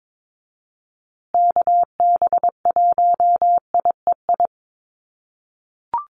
Morse code, heard as a keyed tone, about 22 wpm, 700 Hz. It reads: KB1IEI
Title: USA Call Signs Repeated 2x 22wpm